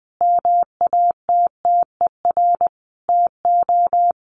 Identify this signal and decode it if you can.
Morse code, heard as a keyed tone, about 20 wpm, 700 Hz. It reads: MATTER TO
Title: Sentences from Top 700 Words 20wpm